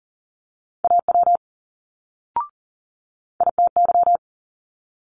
Morse code, heard as a keyed tone, about 39 wpm, 700 Hz. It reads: AW ITY
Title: Multi-letter Phonemes Quick Fire 40wpm